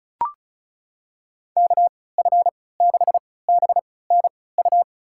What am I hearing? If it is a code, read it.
Morse code, heard as a keyed tone, about 36 wpm, spaced as for 20 wpm, 700 Hz. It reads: KF6BNU
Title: USA Call Signs - 6 Characters - 3x Character Spacing - 35wpm